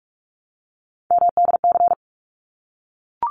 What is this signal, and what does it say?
Morse code, heard as a keyed tone, about 44 wpm, 700 Hz. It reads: MDC
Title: ARRL Sections 45wpm